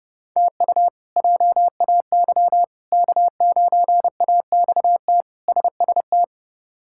Morse code, heard as a keyed tone, about 32 wpm, 700 Hz. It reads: TU JAY K9AXT SST
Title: K1USN SST QSO Practice - 30wpm